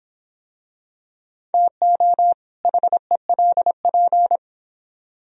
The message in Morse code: TO HELP

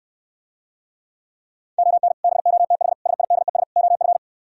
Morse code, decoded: ON BOTH SIDES OF

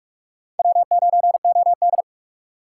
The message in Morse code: W9OD

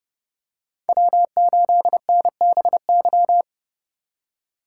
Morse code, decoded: W8NBY

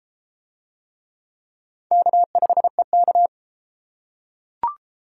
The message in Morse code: K5EK

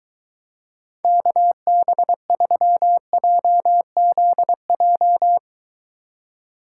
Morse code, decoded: KB3JZJ

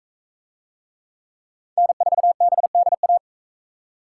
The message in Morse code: N4BDA

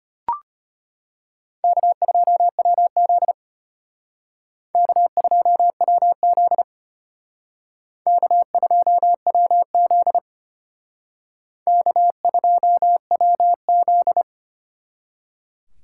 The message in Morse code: K2WZ K2WZ K2WZ K2WZ

